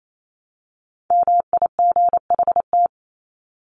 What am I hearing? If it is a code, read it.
Morse code, heard as a keyed tone, about 28 wpm, 700 Hz. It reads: MIGHT
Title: Top 500 Words 28wpm